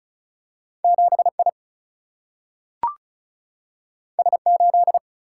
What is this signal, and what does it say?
Morse code, heard as a keyed tone, about 35 wpm, 700 Hz. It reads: 7I S8
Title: Single-Letter Single-Number 35wpm